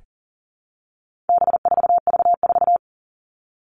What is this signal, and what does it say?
Morse code, heard as a keyed tone, about 40 wpm, 700 Hz. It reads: B4V4
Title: All Cut Numbers - 4-Digit 40wpm